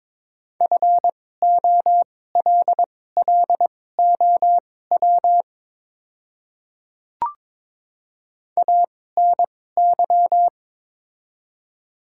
Morse code, heard as a keyed tone, about 22 wpm, spaced as for 16 wpm, 700 Hz. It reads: FOLLOW ANY
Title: 2x Character Spacing - Top 300 Words - 22wpm